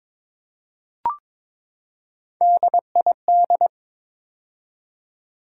Morse code, heard as a keyed tone, about 21 wpm, 700 Hz. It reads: DID
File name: practice / 22wpm / Build A Phrase - 5-Word Phrases - 22wpm